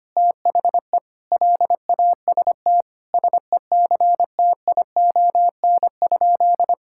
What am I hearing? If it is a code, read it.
Morse code, heard as a keyed tone, about 25 wpm, 700 Hz. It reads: THE LAST SECTION?
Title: Sentences from Top 1000 Words - Review Part 5 - 25wpm